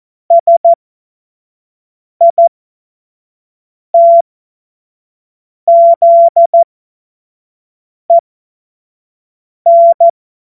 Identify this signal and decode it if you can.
Morse code, heard as a keyed tone, about 14 wpm, spaced as for 5 wpm, 680 Hz. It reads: SITZEN
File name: Morsekurs / Witze / Witz 30. August 2022 25 F70